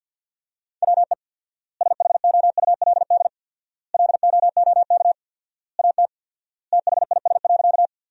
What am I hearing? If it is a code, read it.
Morse code, heard as a keyed tone, about 50 wpm, 700 Hz. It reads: WE SHOULD LOOK AT THIS.
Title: 4x Word-Spacing - Sentences from Top 300 Words - 50wpm